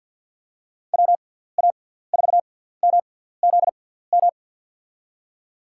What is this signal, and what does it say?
Morse code, heard as a keyed tone, about 53 wpm, spaced as for 17 wpm, 700 Hz. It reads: WA4MZM
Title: USA Call Signs - 6 Characters - 6x Character Spacing - 50wpm